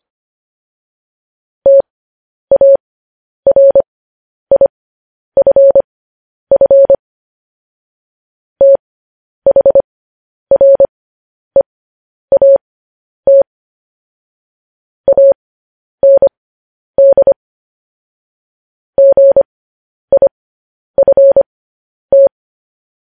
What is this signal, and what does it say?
Morse code, heard as a keyed tone, about 25 wpm, spaced as for 10 wpm, 550 Hz. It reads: TARIFF THREAT AND GIFT
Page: news headlines in morse code at 10 wpm